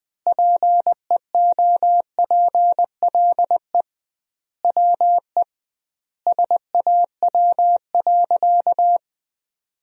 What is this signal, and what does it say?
Morse code, heard as a keyed tone, about 20 wpm, 700 Hz. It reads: PEOPLE WE SAW.
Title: 2x Word-Spacing - Sentences from Top 300 Words - 20wpm